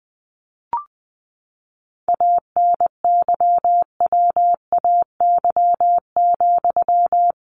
Morse code, heard as a keyed tone, about 19 wpm, 700 Hz. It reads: ANYWAY,
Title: A Sound Of Thunder 20wpm